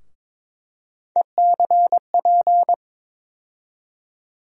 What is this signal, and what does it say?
Morse code, heard as a keyed tone, about 21 wpm, 700 Hz. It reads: ECP
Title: USA Call Sign Postfixes - 3 Letters 22wpm